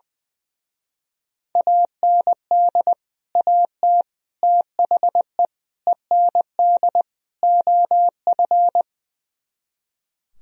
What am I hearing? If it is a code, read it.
Morse code, heard as a keyed tone, about 21 wpm, 700 Hz. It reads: AND AT THE END OF
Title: Speed-Racing - Sets of 5 Words 20wpm